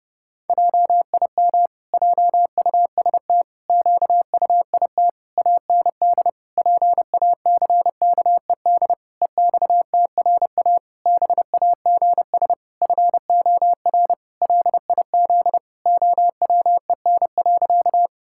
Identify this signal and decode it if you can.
Morse code, heard as a keyed tone, about 30 wpm, 700 Hz. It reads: JIM JUST QUIT AND PACKED EXTRA BAGS FOR LIZ OWEN.
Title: Pangram Challenge v2 - 30wpm